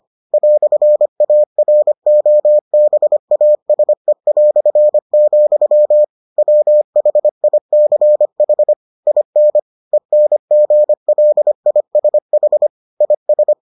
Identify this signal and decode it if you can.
Morse code, heard as a keyed tone, about 25 wpm, 600 Hz. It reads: "AROBASE", WHICH IN ENGLISH IS